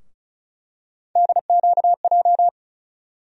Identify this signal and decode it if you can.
Morse code, heard as a keyed tone, about 34 wpm, 700 Hz. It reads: DQJ
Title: USA Call Sign Postfixes - 3 Letters 35wpm